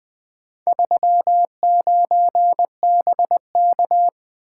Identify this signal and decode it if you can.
Morse code, heard as a keyed tone, about 20 wpm, 700 Hz. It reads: 39BK
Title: Four Letter-Number Combinations - 20wpm